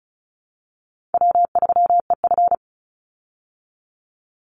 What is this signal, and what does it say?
Morse code, heard as a keyed tone, about 34 wpm, 700 Hz. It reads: W3EF
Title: Call Signs Repeated 3x - 35wpm